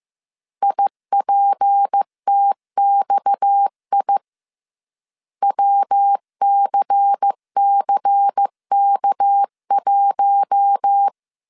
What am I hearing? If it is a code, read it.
Morse code, heard as a keyed tone, about 15 wpm, 800 Hz. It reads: IPTXI WCCK1